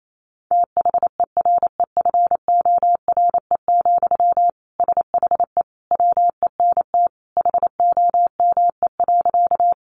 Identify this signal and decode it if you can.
Morse code, heard as a keyed tone, about 28 wpm, 700 Hz. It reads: THEREFORE, SHE WENT HOME.
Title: Sentences from Top 800 Words 28wpm